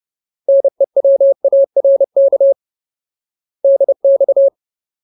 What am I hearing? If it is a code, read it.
Morse code, heard as a keyed tone, about 30 wpm, 550 Hz. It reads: NEWARK DX
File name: Ultimate Word List / 30WPM Ultimate